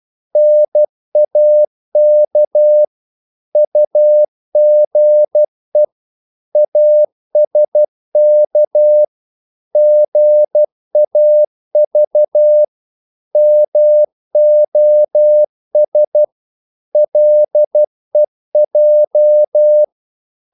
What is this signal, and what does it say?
Morse code, heard as a keyed tone, about 12 wpm, 600 Hz. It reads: NAK UGE ASK GAV MOS LEJ
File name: Korte ord DK 12wpm